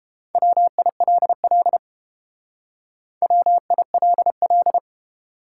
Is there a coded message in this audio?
Yes: Morse code from 0.4 to 4.8 s, heard as a keyed tone, about 29 wpm, 700 Hz. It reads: WILL WILL